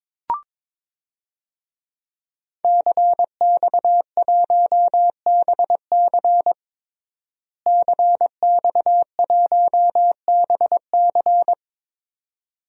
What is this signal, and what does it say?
Morse code, heard as a keyed tone, about 22 wpm, 700 Hz. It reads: CX1BC CX1BC
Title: DXCC-100 Call Signs - 5 Characters - Repeated 2x - 22wpm